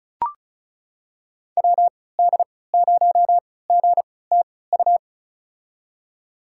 Morse code, heard as a keyed tone, about 36 wpm, spaced as for 20 wpm, 690 Hz. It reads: WD0GTU
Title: USA Call Signs - 3x Character Spacing - 35wpm